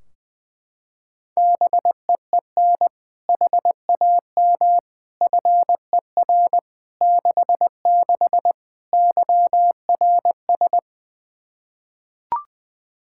Morse code, heard as a keyed tone, about 20 wpm, 700 Hz. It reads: BEEN HAM FER 66 YRS
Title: Practice QSO - Exchange 3 - More Info - 20wpm